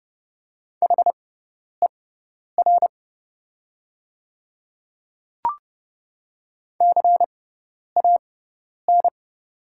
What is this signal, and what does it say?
Morse code, heard as a keyed tone, about 30 wpm, spaced as for 10 wpm, 700 Hz. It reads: HER CAN
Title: 6x Character Spacing - Top 100 Words - 30wpm